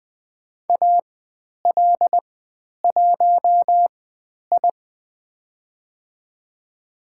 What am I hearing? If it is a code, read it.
Morse code, heard as a keyed tone, about 20 wpm, spaced as for 10 wpm, 700 Hz. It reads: AL1I
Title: 095 Lesson 16g - Letter c - Call Signs 20wpm